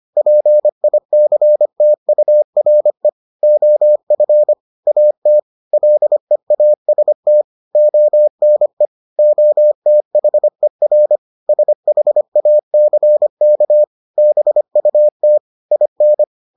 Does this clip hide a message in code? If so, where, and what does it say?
Yes: Morse code from 0.2 to 16.2 s, heard as a keyed tone, about 25 wpm, 600 Hz. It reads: PICTURE OF AT LEAST ONE OTHER SHACK BUT IN